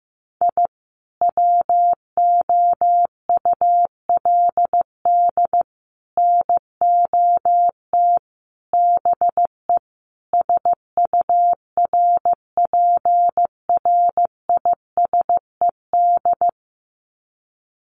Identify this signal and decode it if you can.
Morse code, heard as a keyed tone, about 15 wpm, 700 Hz. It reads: I WOULD NOT BE SURPRISED
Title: Sets of 5 Words Encore2 15wpm